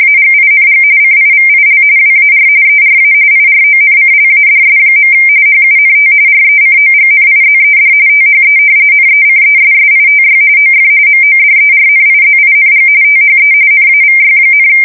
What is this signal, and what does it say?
RTTY, about 45.45 baud. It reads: AGING CIVIL WAR ERA IRONCLADS. HUNT RECOGNIZED THE NECESSITY OF MODERNIZING THE NAVY,
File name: fsk170 / usnavy3